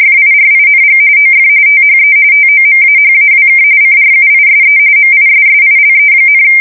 RTTY, about 45.45 baud. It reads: FIRING OF THE TOMAHAWK CRUISE MISSILE,
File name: fsk170 / battleshipnj